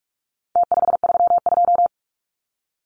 Morse code, heard as a keyed tone, about 44 wpm, 700 Hz. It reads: T532
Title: Numbers - Contest Style - 4-Digit 45wpm